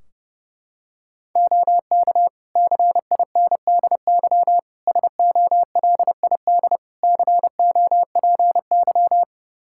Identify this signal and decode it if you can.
Morse code, heard as a keyed tone, about 32 wpm, 700 Hz. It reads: OK CINDY SOLID COPY
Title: K1USN SST QSO Practice 30wpm